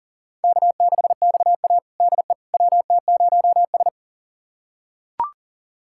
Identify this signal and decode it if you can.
Morse code, heard as a keyed tone, about 40 wpm, 700 Hz. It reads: K6XA DE WT0S
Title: USA Callsign DE Callsign - 4 Characters - 40wpm